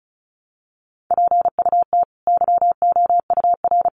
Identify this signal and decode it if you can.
Morse code, heard as a keyed tone, about 34 wpm, 700 Hz. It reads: PUT YOUR
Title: Sentences from Top 1100 Words 35wpm